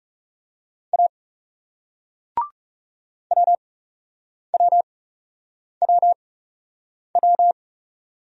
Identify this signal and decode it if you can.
Morse code, heard as a keyed tone, about 40 wpm, spaced as for 8 wpm, 700 Hz. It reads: A WWWW